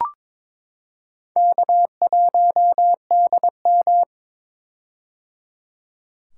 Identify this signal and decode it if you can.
Morse code, heard as a keyed tone, about 23 wpm, 700 Hz. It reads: K1DM
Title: K1USN SST QSO Practice - 22wpm